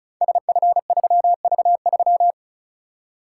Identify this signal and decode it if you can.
Morse code, heard as a keyed tone, about 36 wpm, 700 Hz. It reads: SF3V3
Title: Code Groups - 5 Letter:Numbers - 35wpm